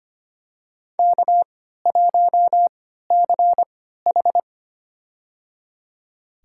Morse code, heard as a keyed tone, about 25 wpm, spaced as for 14 wpm, 700 Hz. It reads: K1CH
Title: USA Call Signs - 4 Characters - 3x Character Spacing - 25wpm